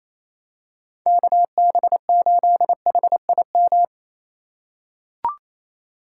Morse code, heard as a keyed tone, about 28 wpm, 700 Hz. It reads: KB8HIM
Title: USA Call Signs - 6 Characters - 2x Character Spacing - 28wpm